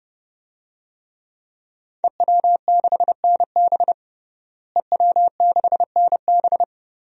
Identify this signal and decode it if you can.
Morse code, heard as a keyed tone, about 30 wpm, 700 Hz. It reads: EW6NB EW6NB
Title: DXCC-100 Call Signs - 5 Characters - Repeated 3x - 30wpm